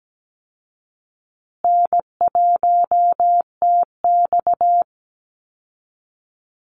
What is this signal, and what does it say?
Morse code, heard as a keyed tone, about 17 wpm, 700 Hz. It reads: N1TX
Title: Call Signs Repeated 3x - 17wpm